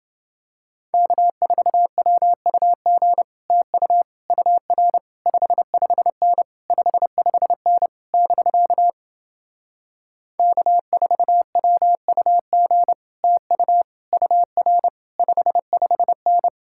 Morse code, heard as a keyed tone, about 30 wpm, 700 Hz. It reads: K4WUG TU UR 55N 55N <BK> K4WUG TU UR 55N